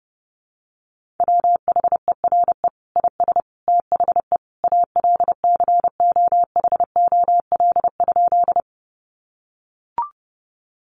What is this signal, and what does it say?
Morse code, heard as a keyed tone, about 30 wpm, 700 Hz. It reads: WHERE IS THE ALCOHOL?
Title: Sentences from Top 1400 Words 30wpm